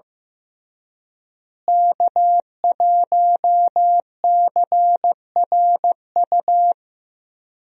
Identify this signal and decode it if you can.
Morse code, heard as a keyed tone, about 15 wpm, 700 Hz. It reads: K1CRU
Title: USA Call Signs - 5 Characters - Repeated 3x - 15wpm